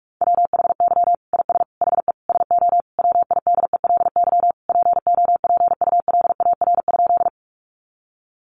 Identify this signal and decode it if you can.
Morse code, heard as a keyed tone, about 44 wpm, 700 Hz. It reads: WHY IS HE SO WIDELY POPULAR?
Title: Sentences from Top 1200 Words 45wpm